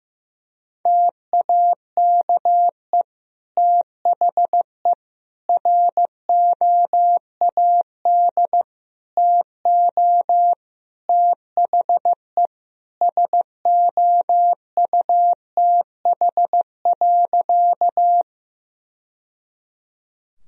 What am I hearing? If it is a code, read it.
Morse code, heard as a keyed tone, about 16 wpm, 700 Hz. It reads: TAKE THE ROAD TO THE SOUTH.
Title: Sentences from Top 1000 Words - Review Part 2 - 15wpm